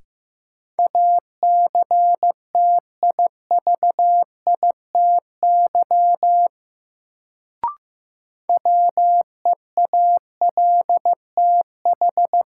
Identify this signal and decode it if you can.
Morse code, heard as a keyed tone, about 15 wpm, 700 Hz. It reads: ACTIVITY WEALTH
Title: Sentences from Top 1000 Words - Review Part 4 - New Word Drill - 15wpm